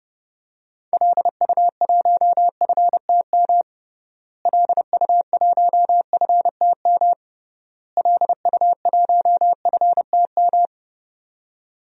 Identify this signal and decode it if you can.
Morse code, heard as a keyed tone, about 32 wpm, 700 Hz. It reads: LU1FTM LU1FTM LU1FTM
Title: DXCC-100 Call Signs - 6 Characters - Repeated 3x - 30wpm